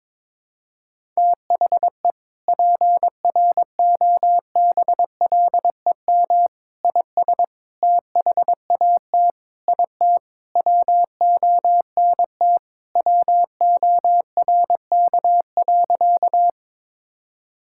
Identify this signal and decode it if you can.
Morse code, heard as a keyed tone, about 22 wpm, 700 Hz. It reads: THE PROBLEM IS THAT IT WONT WORK.
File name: practice / 22wpm / Sentences with 3-Word Phrases - 22wpm